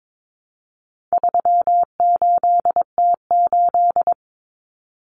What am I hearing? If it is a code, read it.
Morse code, heard as a keyed tone, about 22 wpm, 700 Hz. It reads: 38T8
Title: Common Cut Numbers - 4-Digit 22wpm